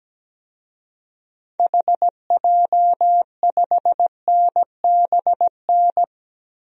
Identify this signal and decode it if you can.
Morse code, heard as a keyed tone, about 17 wpm, 700 Hz. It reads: HJ5NBN
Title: DXCC-100 Call Signs - 6 Characters - Repeated 2x - 17wpm